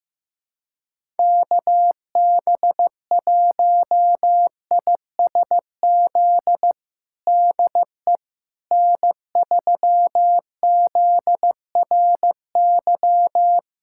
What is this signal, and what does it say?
Morse code, heard as a keyed tone, about 15 wpm, 700 Hz. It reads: KB1ISZ DE N3ZRY
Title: USA Callsign DE Callsign 15wpm